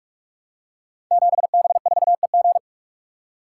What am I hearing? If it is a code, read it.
Morse code, heard as a keyed tone, about 42 wpm, 700 Hz. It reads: 7B4EG
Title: Code Groups - 5 Letter:Numbers - 45wpm